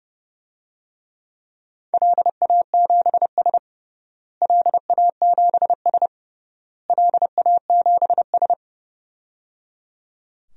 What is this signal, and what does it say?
Morse code, heard as a keyed tone, about 30 wpm, 700 Hz. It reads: LA7S LA7S LA7S
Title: DXCC-100 Call Signs - 4 Characters - Repeated 3x - 30wpm